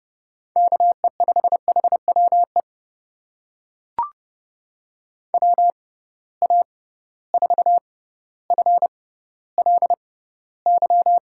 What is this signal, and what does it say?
Morse code, heard as a keyed tone, about 30 wpm, spaced as for 15 wpm, 700 Hz. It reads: KE5HWE WA4FLY
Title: USA Call Signs - 6 Characters - 6x Character Spacing - 30wpm